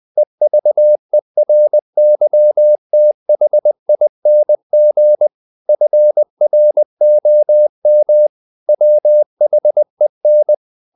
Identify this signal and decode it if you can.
Morse code, heard as a keyed tone, about 20 wpm, 600 Hz. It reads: EVERYTHING FROM WHEN
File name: the old man and the sea / the old man and the sea part 1 20wpm